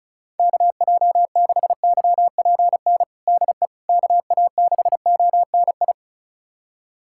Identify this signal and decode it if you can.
Morse code, heard as a keyed tone, about 36 wpm, 700 Hz. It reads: KJ6YPN DE KA6ONI
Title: USA Callsign DE Callsign - 6 Characters - 35wpm